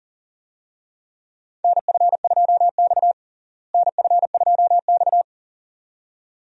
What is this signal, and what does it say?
Morse code, heard as a keyed tone, about 40 wpm, 700 Hz. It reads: NF2X NF2X
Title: USA Call Signs - 4 Characters - Repeated 2x - 40wpm